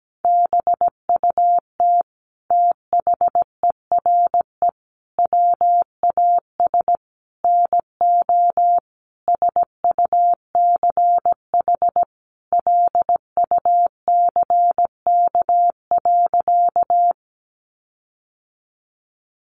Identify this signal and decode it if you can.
Morse code, heard as a keyed tone, about 17 wpm, 700 Hz. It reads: BUT THERE WAS NO SUCH LUCK.
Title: Sentences from Top 1500 Words 17wpm